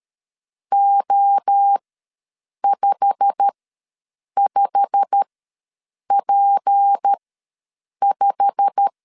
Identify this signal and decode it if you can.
Morse code, heard as a keyed tone, about 13 wpm, spaced as for 7 wpm, 800 Hz. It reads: O55P5
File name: LEZIONI / 8 YJ45 / 8 YJ45 ws 12wpm 3146